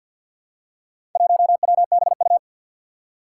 Morse code, heard as a keyed tone, about 51 wpm, 700 Hz. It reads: 1WBU
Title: Four Challenging Letter-Number Combinations 50wpm